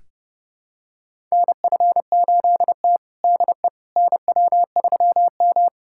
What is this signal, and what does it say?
Morse code, heard as a keyed tone, about 32 wpm, 700 Hz. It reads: NF8T DE NW3M
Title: USA Callsign DE Callsign - 4 Characters - 30wpm